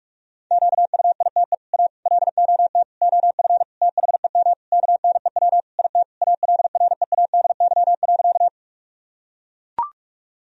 Morse code, heard as a keyed tone, about 42 wpm, 700 Hz. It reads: QUITE A LOT OF THEM KNEW IT ALREADY.
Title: Sentences from Top 1000 Words - Review Part 3 - 45wpm